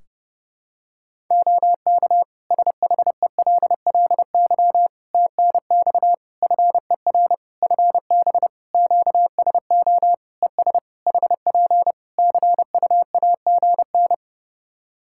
Morse code, heard as a keyed tone, about 30 wpm, 700 Hz. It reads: OK SHELLY TNX FER FB QSO ES HP CUAGN
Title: Practice QSO - Exchange 4 - Ending - 30wpm